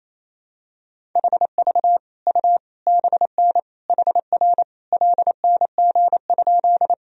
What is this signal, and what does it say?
Morse code, heard as a keyed tone, about 28 wpm, 700 Hz. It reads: HV U BN HR LNG?
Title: Sentences from Top 200 Words and Abbreviations 28wpm